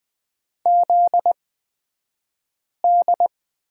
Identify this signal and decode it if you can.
Morse code, heard as a keyed tone, about 20 wpm, spaced as for 5 wpm, 700 Hz. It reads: ZD
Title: 242 Lesson 39d - Letter z - Two Characters 20wpm